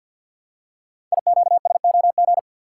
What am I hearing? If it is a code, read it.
Morse code, heard as a keyed tone, about 50 wpm, 690 Hz. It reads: IQSOG
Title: Code Groups - 5 Letters - 50wpm